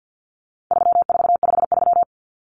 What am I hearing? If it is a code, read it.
Morse code, heard as a keyed tone, about 50 wpm, 700 Hz. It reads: 3453
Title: Numbers - 4-Digit 50wpm